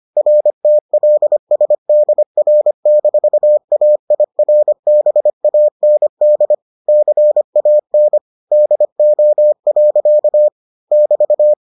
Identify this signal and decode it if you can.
Morse code, heard as a keyed tone, about 25 wpm, 600 Hz. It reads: RTLSDR-AIRBAND CAN DO. <BT>